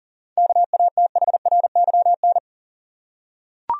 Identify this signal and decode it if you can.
Morse code, heard as a keyed tone, about 40 wpm, 700 Hz. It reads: KATHRYN
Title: USA Common Names - 40wpm